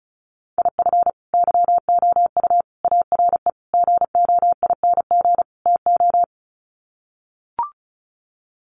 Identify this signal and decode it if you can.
Morse code, heard as a keyed tone, about 34 wpm, 700 Hz. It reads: IF YOU ARE GOING TO
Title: Sets of 5 Words Encore2 35wpm